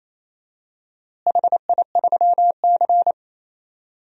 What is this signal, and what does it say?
Morse code, heard as a keyed tone, about 28 wpm, 700 Hz. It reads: HI3C
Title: DXCC-100 Call Signs - 4 Characters - Repeated 2x - 28wpm